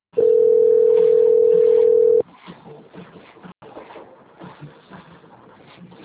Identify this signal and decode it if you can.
Ringback tone (North America), 440 + 480 Hz.